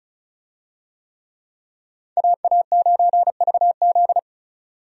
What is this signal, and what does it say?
Morse code, heard as a keyed tone, about 36 wpm, 700 Hz. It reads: AA9VZ
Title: USA Call Signs - 5 Characters - 35wpm